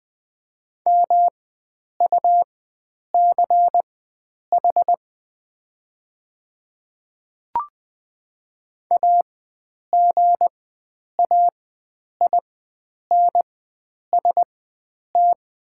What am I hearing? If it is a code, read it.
Morse code, heard as a keyed tone, about 20 wpm, spaced as for 9 wpm, 690 Hz. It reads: MUCH AGAINST
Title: 4x Character Spacing - Top 500 Words - 20wpm